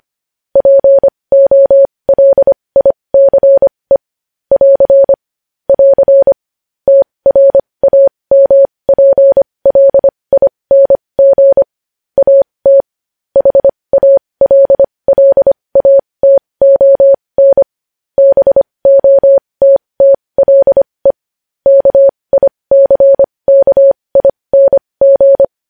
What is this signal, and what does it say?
Morse code, heard as a keyed tone, about 25 wpm, spaced as for 20 wpm, 550 Hz. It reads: POLICE <AR> <AR> TRAMPLING AT HALLATON BOTTLE KICKING
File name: podcast / News headlines-1x-25x20